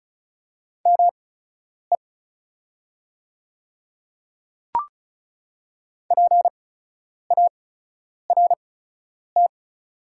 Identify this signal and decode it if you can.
Morse code, heard as a keyed tone, about 35 wpm, spaced as for 10 wpm, 690 Hz. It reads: ME PART
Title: 8x Character Spacing - Top 200 Words - 35wpm